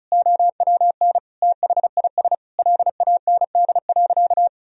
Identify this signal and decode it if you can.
Morse code, heard as a keyed tone, about 36 wpm, 700 Hz. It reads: OWN THIS LAND.
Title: Sentences from Top 1000 Words - Review Part 4 - 35wpm